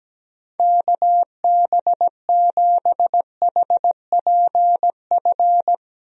Morse code, heard as a keyed tone, about 17 wpm, 700 Hz. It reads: KB7HPF
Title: USA Call Signs - 6 Characters - Repeated 2x - 17wpm